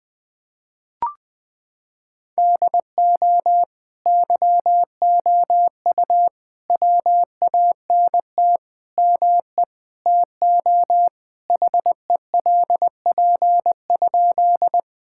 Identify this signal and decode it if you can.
Morse code, heard as a keyed tone, about 20 wpm, 700 Hz. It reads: DO YOU WANT ME TO HELP?
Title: Sentences with 4-Word Phrases - 20wpm